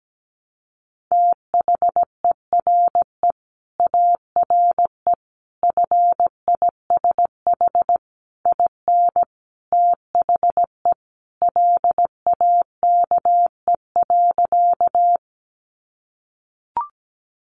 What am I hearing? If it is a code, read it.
Morse code, heard as a keyed tone, about 17 wpm, 700 Hz. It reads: THERE ARE FISH IN THE LAKE.
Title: Sentences from Top 1100 Words 17wpm